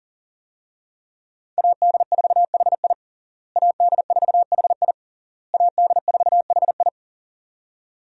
Morse code, heard as a keyed tone, about 37 wpm, 700 Hz. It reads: AD4HI AD4HI AD4HI
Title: USA Call Signs - 5 Characters - Repeated 3x - 40wpm